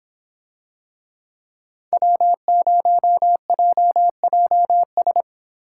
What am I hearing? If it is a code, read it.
Morse code, heard as a keyed tone, about 26 wpm, 700 Hz. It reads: W0JJS